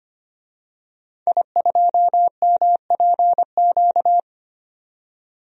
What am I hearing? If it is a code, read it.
Morse code, heard as a keyed tone, about 25 wpm, 700 Hz. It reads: I2MPQ
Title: DXCC-100 Call Signs - 5 Characters - 25wpm